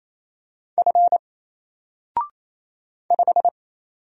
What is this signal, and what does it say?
Morse code, heard as a keyed tone, about 27 wpm, 700 Hz. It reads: F 5
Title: Single Letter-Number 28wpm